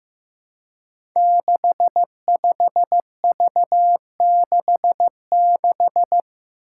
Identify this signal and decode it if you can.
Morse code, heard as a keyed tone, about 15 wpm, 700 Hz. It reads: 65V66
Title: Five Challenging Letter-Number Combinations 15wpm